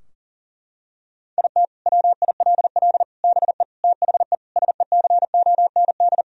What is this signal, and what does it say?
Morse code, heard as a keyed tone, about 40 wpm, 700 Hz. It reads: IT WILL BE THE SECOND
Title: Sentences from Top 1000 Words - Review Part 2 - 40wpm